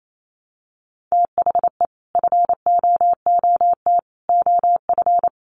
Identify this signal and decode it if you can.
Morse code, heard as a keyed tone, about 28 wpm, 700 Hz. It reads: THE FOOT OF
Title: Sets of 5 Words Encore2 28wpm